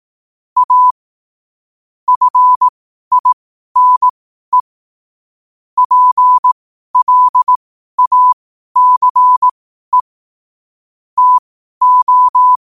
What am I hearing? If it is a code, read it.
Morse code, heard as a keyed tone, about 18 wpm, spaced as for 12 wpm, 1000 Hz. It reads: A FINE PLACE TO
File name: qotdfiles / 12wpm